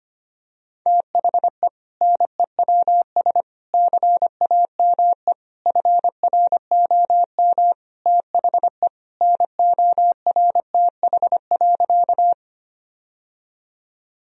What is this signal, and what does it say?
Morse code, heard as a keyed tone, about 25 wpm, 700 Hz. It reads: THE NEWS CAME FROM THE NORTH.
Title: Sentences from Top 1000 Words - Review Part 1 - 25wpm